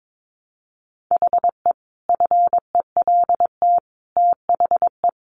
Morse code, heard as a keyed tone, about 22 wpm, 700 Hz. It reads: HE FELT THE
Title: A Sound Of Thunder 22wpm